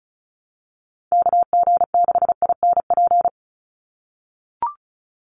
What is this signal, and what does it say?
Morse code, heard as a keyed tone, about 34 wpm, 700 Hz. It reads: KG6INP
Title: USA Call Signs 35wpm